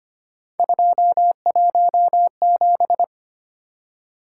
Morse code, heard as a keyed tone, about 25 wpm, 700 Hz. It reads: 217
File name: Three Letter-Number Combinations - 25wpm